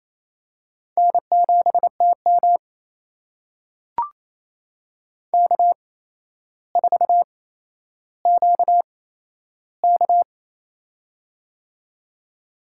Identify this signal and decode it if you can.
Morse code, heard as a keyed tone, about 28 wpm, spaced as for 12 wpm, 700 Hz. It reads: N7TM K4QK
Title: USA Call Signs - 4 Characters - 8x Character Spacing - 28wpm